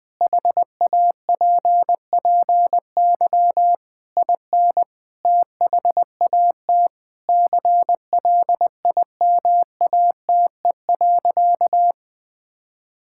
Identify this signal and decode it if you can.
Morse code, heard as a keyed tone, about 20 wpm, 700 Hz. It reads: HAPPY IN THAT CLIMATE.
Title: Sentences from Top 1600 Words 20wpm